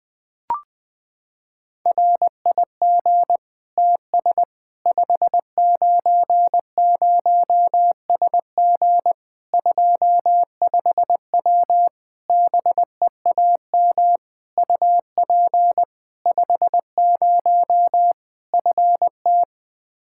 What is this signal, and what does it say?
Morse code, heard as a keyed tone, about 20 wpm, 700 Hz. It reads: RIG TS 590SG 25W BEAM UP 50 FT
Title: Practice QSO - Exchange 2 - Info - 20wpm